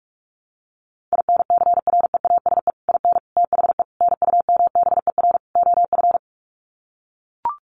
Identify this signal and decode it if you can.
Morse code, heard as a keyed tone, about 44 wpm, 700 Hz. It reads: INCREASE IN THE NUMBER OF
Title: Sets of 5 Words 45wpm